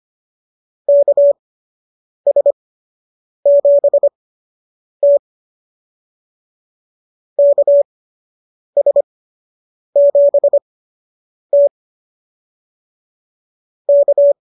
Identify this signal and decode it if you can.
Morse code, heard as a keyed tone, about 25 wpm, spaced as for 8 wpm, 590 Hz. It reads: KS7T KS7T K